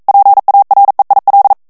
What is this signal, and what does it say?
Morse code, heard as a keyed tone, about 42 wpm, 800 Hz. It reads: PAREIL
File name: CW / FICHIERS-MP3 / FRANCAIS TEXTE CLAIR 2 VITESSE 40 MOTS MINUTES